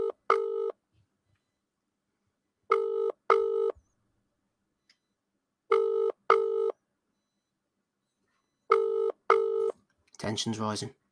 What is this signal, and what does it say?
Ringback tone (UK), 400 + 450 Hz, with speech.